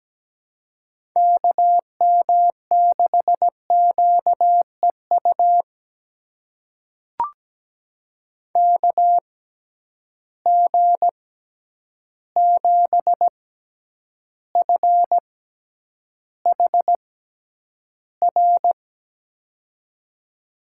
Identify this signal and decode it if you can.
Morse code, heard as a keyed tone, about 17 wpm, spaced as for 9 wpm, 700 Hz. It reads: KM6QEU KG7FHR
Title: USA Call Signs - 6 Characters - 6x Character Spacing - 17wpm